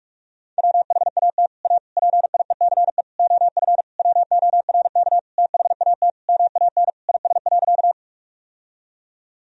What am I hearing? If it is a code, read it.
Morse code, heard as a keyed tone, about 41 wpm, 700 Hz. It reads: WHAT A PIECE OF WORK THAT MAN IS.
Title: Sentences from Top 1000 Words - Review Part 2 - 45wpm